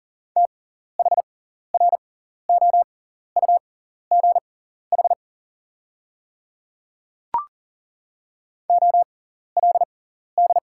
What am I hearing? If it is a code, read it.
Morse code, heard as a keyed tone, about 40 wpm, spaced as for 14 wpm, 690 Hz. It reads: THROUGH OLD
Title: 6x Character Spacing - Top 300 Words - 40wpm